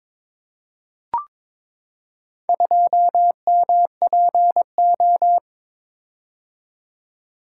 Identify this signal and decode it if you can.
Morse code, heard as a keyed tone, about 21 wpm, 700 Hz. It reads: 2MPO
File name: practice / 22wpm / Four Letter-Number Combinations - 22wpm